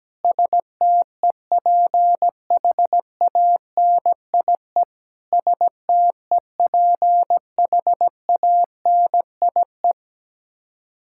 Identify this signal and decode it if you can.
Morse code, heard as a keyed tone, about 17 wpm, 700 Hz. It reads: STEPHANIE STEPHANIE
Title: Practice QSO - Exchange 1 - Beginning - 17wpm